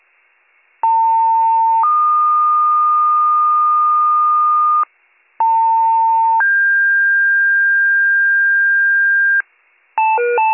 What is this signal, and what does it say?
Two-tone page, heard as a one-second tone, then a three-second tone, about 900 then 1220 Hz, about 900 then 1600 Hz, with noise. It is heard from 0.8 to 9.4 s.